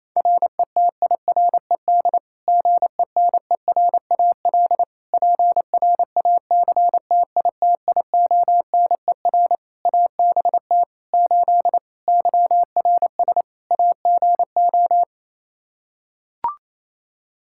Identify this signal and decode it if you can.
Morse code, heard as a keyed tone, about 28 wpm, 700 Hz. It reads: RETIRED GENERAL PRACTITIONER ABT 8 YRS AGO
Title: Practice QSO - Exchange 3 - More Info - 28wpm